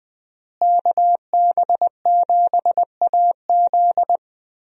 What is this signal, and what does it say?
Morse code, heard as a keyed tone, about 21 wpm, 700 Hz. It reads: KB7AZ
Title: USA Call Signs - 5 Characters - Repeated 3x - 20wpm